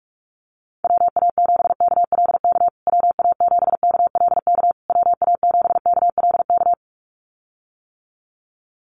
Morse code, heard as a keyed tone, about 44 wpm, 700 Hz. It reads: WA7KLK WA7KLK WA7KLK
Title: USA Call Signs Repeated 3x 45wpm